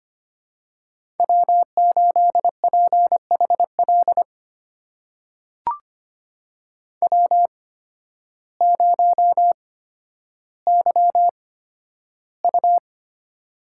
Morse code, heard as a keyed tone, about 25 wpm, spaced as for 12 wpm, 700 Hz. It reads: W8PHL W0YU
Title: USA Call Signs - 8x Character Spacing - 25wpm